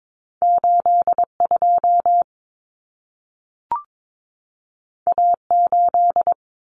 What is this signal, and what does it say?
Morse code, heard as a keyed tone, about 22 wpm, 700 Hz. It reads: 82 A8
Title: Common Cut Numbers - 2-Digit 22wpm